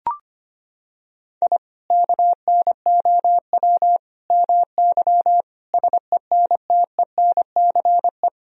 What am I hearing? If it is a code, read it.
Morse code, heard as a keyed tone, about 25 wpm, 700 Hz. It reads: I KNOW MY SENTENCE
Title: Sentences from Top 1000 Words - Review Part 1 - 25wpm